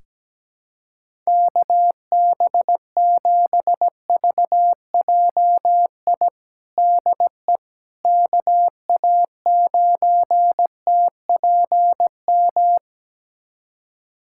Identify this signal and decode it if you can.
Morse code, heard as a keyed tone, about 17 wpm, 700 Hz. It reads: KB7VJI DE KA9TPM
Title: USA Callsign DE Callsign - 6 Characters - 17wpm